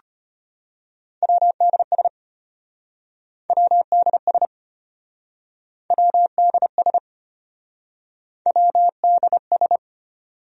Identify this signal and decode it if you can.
Morse code, heard as a keyed tone, about 33 wpm, 700 Hz. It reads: WDS WDS WDS WDS